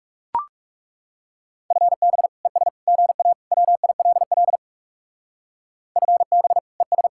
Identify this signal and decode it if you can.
Morse code, heard as a keyed tone, about 38 wpm, 700 Hz. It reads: FB ES GA WILL FB ES